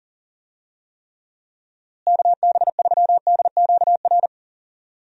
Morse code, heard as a keyed tone, about 40 wpm, 700 Hz. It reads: KB3DQR
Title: USA Call Signs - 6 Characters - Repeated 3x - 40wpm